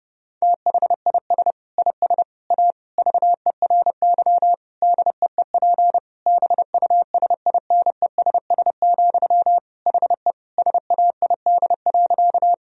Morse code, heard as a keyed tone, about 30 wpm, 700 Hz. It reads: THIS IS A VERY DEEP BUSINESS, HE SAID.
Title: Sentences from Top 1000 Words - Review Part 3 - 30wpm